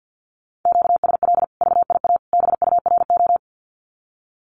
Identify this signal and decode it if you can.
Morse code, heard as a keyed tone, about 50 wpm, 700 Hz. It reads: QSL VIA BURO
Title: Common QSO Phrases 50wpm